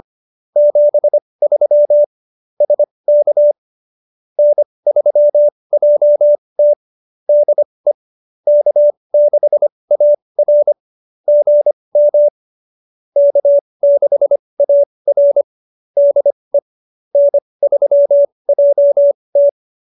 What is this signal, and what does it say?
Morse code, heard as a keyed tone, about 25 wpm, spaced as for 20 wpm, 570 Hz. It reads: 73 SK N3JT DE K6AR GM K6AR DE N3JT
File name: QSO307 20